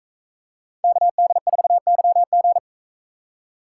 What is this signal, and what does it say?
Morse code, heard as a keyed tone, about 41 wpm, 700 Hz. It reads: KD4YG